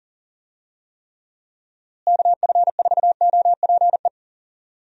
Morse code, heard as a keyed tone, about 40 wpm, 700 Hz. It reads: KF4OPE
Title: USA Call Signs - 6 Characters - Repeated 2x - 40wpm